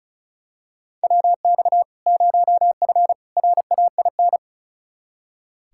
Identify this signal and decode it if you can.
Morse code, heard as a keyed tone, about 36 wpm, 700 Hz. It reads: WX 0F RAIN
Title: Practice QSO - Exchange 2 - Info - 35wpm